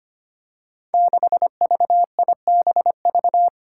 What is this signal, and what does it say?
Morse code, heard as a keyed tone, about 25 wpm, 700 Hz. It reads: 6VIBV
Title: Five Challenging Letter-Number Combinations 25wpm